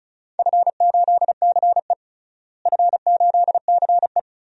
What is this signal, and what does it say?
Morse code, heard as a keyed tone, about 34 wpm, 700 Hz. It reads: F8CE F8CE
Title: DXCC-100 Call Signs - 4 Characters - Repeated 3x - 35wpm